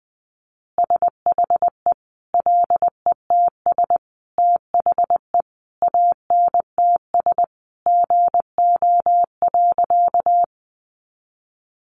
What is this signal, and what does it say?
Morse code, heard as a keyed tone, about 20 wpm, 700 Hz. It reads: SHE LETS THE ANTS GO.
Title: Additional Easy Phrases 20wpm